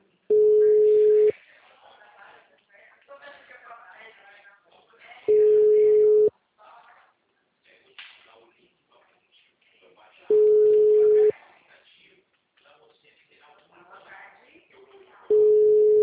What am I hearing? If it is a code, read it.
Ringback tone (Europe, 425 Hz), with speech.